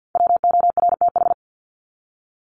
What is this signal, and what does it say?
Morse code, heard as a keyed tone, about 50 wpm, 700 Hz. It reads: WORTH
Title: Top 501-750 Words 50wpm